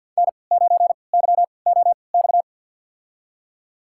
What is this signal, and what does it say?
Morse code, heard as a keyed tone, about 51 wpm, spaced as for 28 wpm, 700 Hz. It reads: N9YOX
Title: USA Call Signs - 5 Characters - 3x Character Spacing - 50wpm